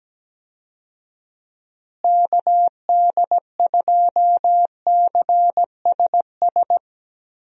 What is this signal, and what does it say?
Morse code, heard as a keyed tone, about 17 wpm, 700 Hz. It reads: KD2CSS
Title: USA Call Signs - 6 Characters - Repeated 2x - 17wpm